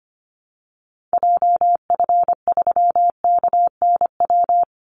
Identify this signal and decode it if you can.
Morse code, heard as a keyed tone, about 25 wpm, 700 Hz. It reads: JF3KNW
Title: Call Signs Repeated 3x - 25wpm